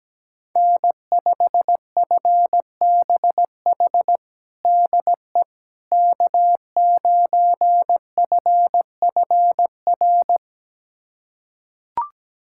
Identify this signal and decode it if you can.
Morse code, heard as a keyed tone, about 17 wpm, 700 Hz. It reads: N5FBH DE K9FFR
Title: USA Callsign DE Callsign - 5 Characters - 17wpm